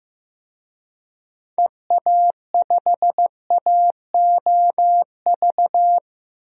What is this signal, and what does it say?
Morse code, heard as a keyed tone, about 15 wpm, 700 Hz. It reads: EA5AOV
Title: DXCC-100 Call Signs - 6 Characters - 15wpm